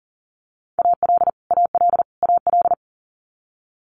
Morse code, heard as a keyed tone, about 41 wpm, 700 Hz. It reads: AL AL AL
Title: Multi-letter Phonemes Repeated 3x 40wpm